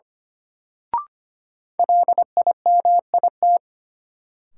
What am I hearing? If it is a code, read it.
Morse code, heard as a keyed tone, about 25 wpm, 700 Hz. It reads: LIMIT
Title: Sentences from Top 1500 Words - New Word Drill Alt - 25wpm